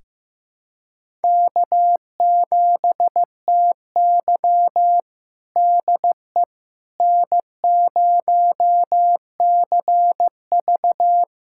Morse code, heard as a keyed tone, about 15 wpm, 700 Hz. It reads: K7TY DE N0CV
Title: USA Callsign DE Callsign - 4 Characters - 15wpm